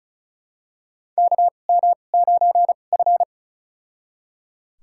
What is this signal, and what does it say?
Morse code, heard as a keyed tone, about 36 wpm, spaced as for 25 wpm, 700 Hz. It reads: KM9F
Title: USA Call Signs - 4 Characters - 2x Character Spacing - 35wpm